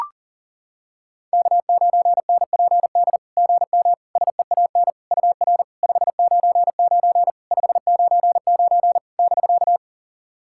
Morse code, heard as a keyed tone, about 40 wpm, 690 Hz. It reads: K9NPD GM SEAN UR 599 599 <BK>
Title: SOTA QSO Practice 40wpm